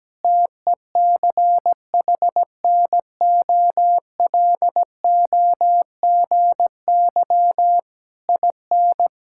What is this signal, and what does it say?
Morse code, heard as a keyed tone, about 17 wpm, 700 Hz. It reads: TECHNOLOGY IN
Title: Sentences from Top 1000 Words - Review Part 5 - 17wpm